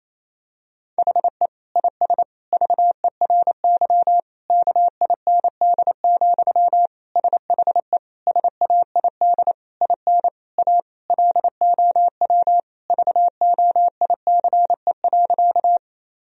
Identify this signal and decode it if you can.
Morse code, heard as a keyed tone, about 28 wpm, 700 Hz. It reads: HE IS VERY KIND, SHE SAID IN A LOW VOICE.
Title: Sentences from Top 1000 Words - Review Part 1 - 28wpm